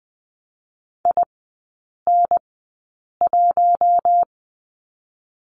Morse code, heard as a keyed tone, about 20 wpm, spaced as for 8 wpm, 700 Hz. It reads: IN1
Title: 041 Lesson 8e - Number 1 - Three Characters 20wpm